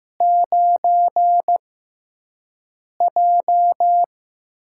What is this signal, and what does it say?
Morse code, heard as a keyed tone, about 15 wpm, spaced as for 5 wpm, 700 Hz. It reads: 9J
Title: USA Call Signs - 4 Characters - 6x Character Spacing - 15wpm